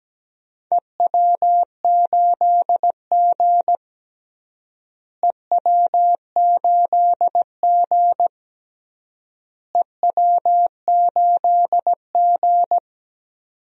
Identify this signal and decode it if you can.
Morse code, heard as a keyed tone, about 17 wpm, 700 Hz. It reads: EW8G EW8G EW8G
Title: DXCC-100 Call Signs - 4 Characters - Repeated 3x - 17wpm